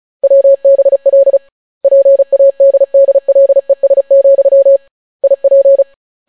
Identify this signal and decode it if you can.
Morse code, heard as a keyed tone, about 36 wpm, 550 Hz. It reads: WBL PADDLES, IP